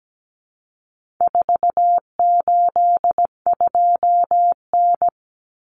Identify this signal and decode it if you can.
Morse code, heard as a keyed tone, about 17 wpm, 700 Hz. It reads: 482N
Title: Common Cut Numbers - 4-Digit 17wpm